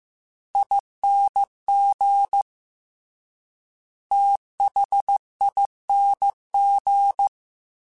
Morse code, heard as a keyed tone, about 15 wpm, 800 Hz. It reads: ING THING
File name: Special INGs / Special INGs-mp3s / Slow ing @13